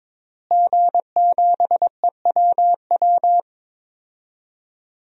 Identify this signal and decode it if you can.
Morse code, heard as a keyed tone, about 21 wpm, 700 Hz. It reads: G7EWW
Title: DXCC-100 Call Signs - 5 Characters - Repeated 2x - 22wpm